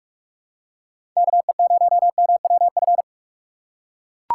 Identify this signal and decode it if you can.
Morse code, heard as a keyed tone, about 43 wpm, 700 Hz. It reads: KE0MWF
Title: USA Call Signs - 2x Character Spacing - 45wpm